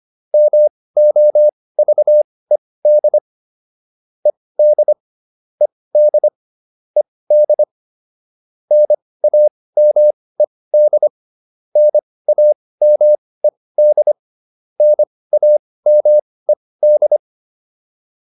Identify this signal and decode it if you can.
Morse code, heard as a keyed tone, about 25 wpm, spaced as for 18 wpm, 600 Hz. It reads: MOVED ED ED ED NAMED NAMED NAMED